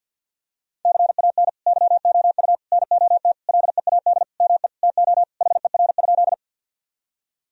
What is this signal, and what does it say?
Morse code, heard as a keyed tone, about 50 wpm, 700 Hz. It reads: CAN YOU NOT LEAD ME TO HER?
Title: Sentences from Top 1000 Words - Review Part 2 - 50wpm